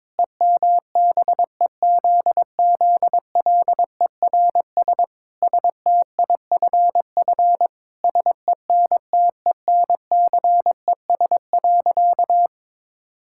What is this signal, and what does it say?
Morse code, heard as a keyed tone, about 22 wpm, 700 Hz. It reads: EMBEZZLERS STIFF SENTENCES.
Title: Pangram Challenge v2 - 22wpm